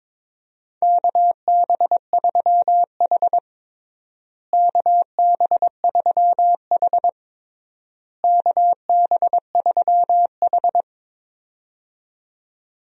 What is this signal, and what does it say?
Morse code, heard as a keyed tone, about 22 wpm, 700 Hz. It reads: KB3H KB3H KB3H
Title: USA Call Signs - 4 Characters - Repeated 3x - 22wpm